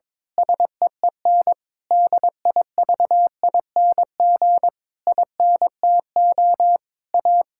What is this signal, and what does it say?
Morse code, heard as a keyed tone, about 22 wpm, 700 Hz. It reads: SEEN DIVING INTO A
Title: Pangram Challenge v2 - 22wpm